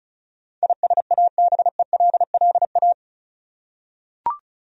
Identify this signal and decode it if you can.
Morse code, heard as a keyed tone, about 36 wpm, 700 Hz. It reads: ISABELLA
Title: USA Common Names - 35wpm